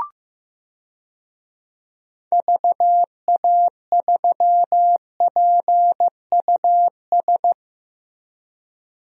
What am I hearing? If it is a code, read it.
Morse code, heard as a keyed tone, about 15 wpm, 700 Hz. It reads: VA3PUS
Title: DXCC-100 Call Signs - 6 Characters - Repeated 2x - 15wpm